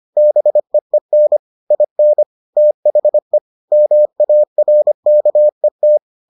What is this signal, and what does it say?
Morse code, heard as a keyed tone, about 25 wpm, 600 Hz. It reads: BEEN IN THE MARKET